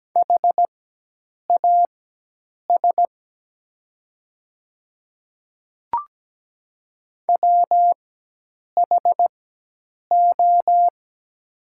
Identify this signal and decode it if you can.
Morse code, heard as a keyed tone, about 17 wpm, spaced as for 8 wpm, 700 Hz. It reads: HAS WHO
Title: 4x Character Spacing - Top 200 Words - 17wpm